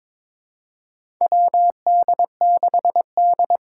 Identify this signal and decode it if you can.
Morse code, heard as a keyed tone, about 23 wpm, 700 Hz. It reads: WD6D
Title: USA Call Signs - 8x Character Spacing - 22wpm